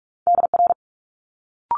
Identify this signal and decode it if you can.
Morse code, heard as a keyed tone, about 46 wpm, 700 Hz. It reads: DR
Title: Multi-letter Phonemes Quick Fire 45wpm